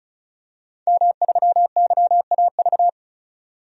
Morse code, heard as a keyed tone, about 34 wpm, 700 Hz. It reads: M3YAV
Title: DXCC-100 Call Signs - 5 Characters - 35wpm